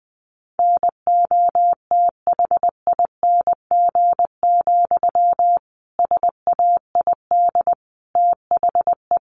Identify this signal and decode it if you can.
Morse code, heard as a keyed tone, about 20 wpm, 700 Hz. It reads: NOTHING, SAID THE
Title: Sentences from Top 1300 Words 20wpm